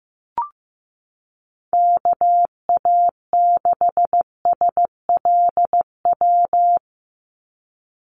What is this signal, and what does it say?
Morse code, heard as a keyed tone, about 15 wpm, 700 Hz. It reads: KA6SLW
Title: USA Call Signs 15wpm